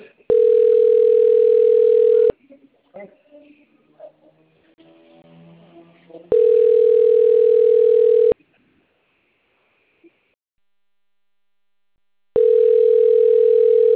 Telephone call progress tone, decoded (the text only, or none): ringback tone